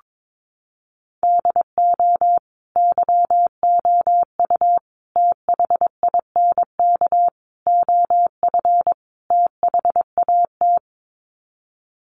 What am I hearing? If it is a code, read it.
Morse code, heard as a keyed tone, about 22 wpm, 700 Hz. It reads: DO YOU THINK OF THAT
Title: Sets of 5 Words Encore2 22wpm